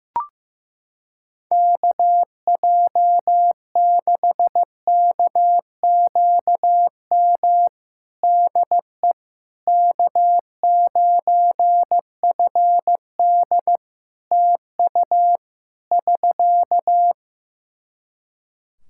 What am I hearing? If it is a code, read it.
Morse code, heard as a keyed tone, about 15 wpm, 700 Hz. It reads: KJ6KQM DE K9FD TU <SK>
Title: K1USN SST QSO Practice 15wpm